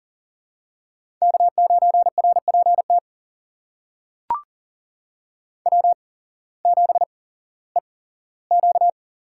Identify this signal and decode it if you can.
Morse code, heard as a keyed tone, about 40 wpm, spaced as for 19 wpm, 700 Hz. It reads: K9RPT W7EQ
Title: USA Call Signs - 8x Character Spacing - 40wpm